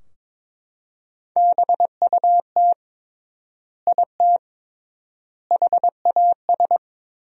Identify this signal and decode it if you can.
Morse code, heard as a keyed tone, about 22 wpm, 700 Hz. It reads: BUT IT HAS
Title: 3x Word-Spacing - Sentences from Top 100 Words - 22wpm